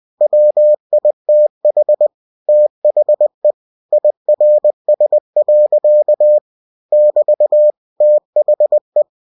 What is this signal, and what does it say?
Morse code, heard as a keyed tone, about 20 wpm, 600 Hz. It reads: WITH THE IRS. <BT> THE